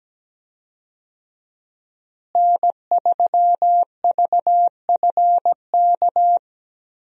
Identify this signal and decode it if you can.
Morse code, heard as a keyed tone, about 17 wpm, 700 Hz. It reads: N3VFK
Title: USA Call Signs - 5 Characters - Repeated 2x - 17wpm